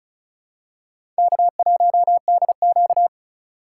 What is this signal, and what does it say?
Morse code, heard as a keyed tone, about 34 wpm, 700 Hz. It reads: K1DQ
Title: USA Call Signs - 4 Characters - 3x Character Spacing - 35wpm